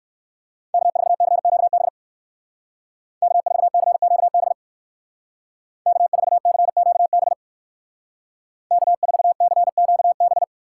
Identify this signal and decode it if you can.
Morse code, heard as a keyed tone, about 59 wpm, 700 Hz. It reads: K4CQB K4CQB K4CQB K4CQB